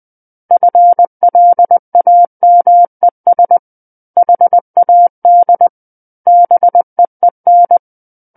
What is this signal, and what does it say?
Morse code, heard as a keyed tone, about 20 wpm, 700 Hz. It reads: FLAMES HAD BEEN
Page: SKCC Learning Center - Morse Code Practice Files